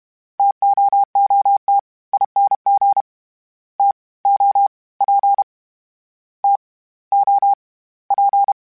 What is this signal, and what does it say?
Morse code, heard as a keyed tone, about 32 wpm, 800 Hz. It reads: TOOT ING T O P T O P